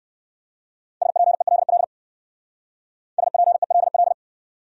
Morse code, heard as a keyed tone, about 57 wpm, 700 Hz. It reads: SPELL SPELL